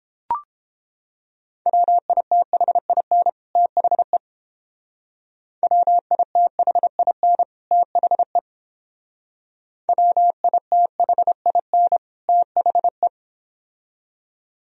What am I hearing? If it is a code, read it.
Morse code, heard as a keyed tone, about 30 wpm, 700 Hz. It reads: WITHIN THE WITHIN THE WITHIN THE